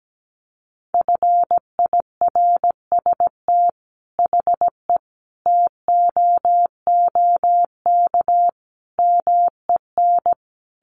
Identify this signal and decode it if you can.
Morse code, heard as a keyed tone, about 17 wpm, 700 Hz. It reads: FIRST HE TOOK MEN
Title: Sentences from Top 1500 Words 17wpm